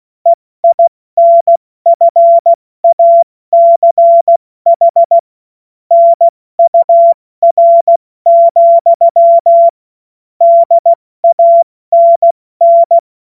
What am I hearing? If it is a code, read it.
Morse code, heard as a keyed tone, about 16 wpm, 680 Hz. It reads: EINFACH NUR, DANN
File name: Morsekurs / Witze / Witz 04. Juli 2023 70 F80